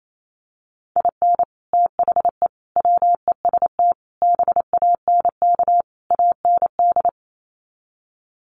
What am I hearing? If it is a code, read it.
Morse code, heard as a keyed tone, about 28 wpm, 700 Hz. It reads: IN THE WEST BANK AND
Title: Sets of 5 Words Encore2 28wpm